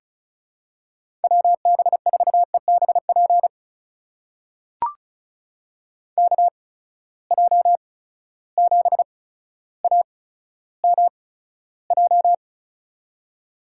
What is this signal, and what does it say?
Morse code, heard as a keyed tone, about 34 wpm, spaced as for 15 wpm, 700 Hz. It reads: WB4EBP KJ7AMJ
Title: USA Call Signs - 6 Characters - 8x Character Spacing - 35wpm